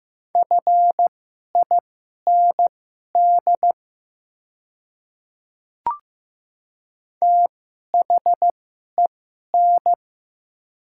Morse code, heard as a keyed tone, about 15 wpm, spaced as for 11 wpm, 690 Hz. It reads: FIND THEN
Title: 2x Character Spacing - Top 100 Words - 15wpm